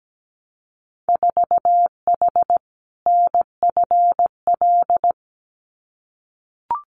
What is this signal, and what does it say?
Morse code, heard as a keyed tone, about 17 wpm, 700 Hz. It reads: 4H NFL
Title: Winter Field Day Exchanges Encore 17wpm